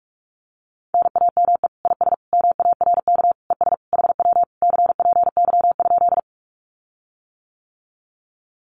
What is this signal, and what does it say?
Morse code, heard as a keyed tone, about 44 wpm, 700 Hz. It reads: NAME IS MARK ES HW CPY?
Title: Common QSO Phrases Encore 45wpm